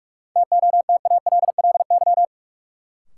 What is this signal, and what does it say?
Morse code, heard as a keyed tone, about 46 wpm, 700 Hz. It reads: TOTALLY
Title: Sentences from Top 1600 Words - New Word Drill Alt - 45wpm